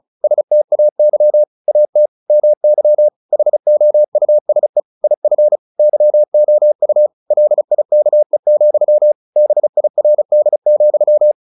Morse code, heard as a keyed tone, about 34 wpm, 600 Hz. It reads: STAY AT MY HOUSE IF YOU LIKE, BIRD,
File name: the old man and the sea / the old man and the sea 1 35wpm